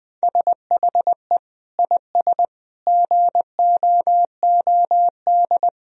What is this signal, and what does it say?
Morse code, heard as a keyed tone, about 20 wpm, 700 Hz. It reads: SHE IS GOOD
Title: Sentences from Top 1000 Words - Review Part 3 - 20wpm